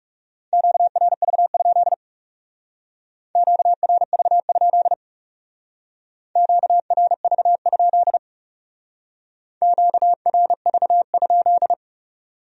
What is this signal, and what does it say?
Morse code, heard as a keyed tone, about 37 wpm, 700 Hz. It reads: QRV? QRV? QRV? QRV?
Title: Speed-Racing Q-Codes - 30wpm